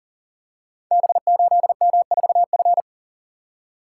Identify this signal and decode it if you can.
Morse code, heard as a keyed tone, about 42 wpm, 690 Hz. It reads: B8M4F
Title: Code Groups - 5 Letter:Numbers - 40wpm